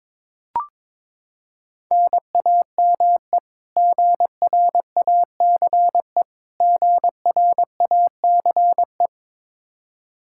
Morse code, heard as a keyed tone, about 22 wpm, 700 Hz. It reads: NAME GRACE GRACE
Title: Practice QSO - Exchange 1 - Beginning - 22wpm